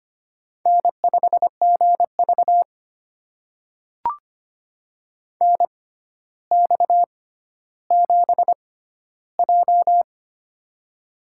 Morse code, heard as a keyed tone, about 25 wpm, spaced as for 13 wpm, 700 Hz. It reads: N5GV NX7J
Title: USA Call Signs - 6x Character Spacing - 25wpm